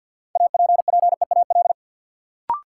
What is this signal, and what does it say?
Morse code, heard as a keyed tone, about 53 wpm, 700 Hz. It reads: APPEAL
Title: Sentences from Top 1300 Words - New Word Drill Alt - 50wpm